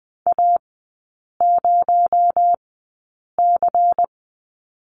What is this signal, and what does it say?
Morse code, heard as a keyed tone, about 20 wpm, spaced as for 8 wpm, 700 Hz. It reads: A0C
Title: 218 Lesson 35e - Number 0 - Three Characters 20wpm